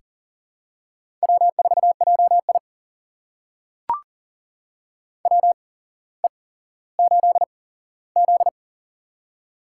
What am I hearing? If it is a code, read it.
Morse code, heard as a keyed tone, about 42 wpm, spaced as for 17 wpm, 700 Hz. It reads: W4JI WE8Z
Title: USA Call Signs - 4 Characters - 8x Character Spacing - 40wpm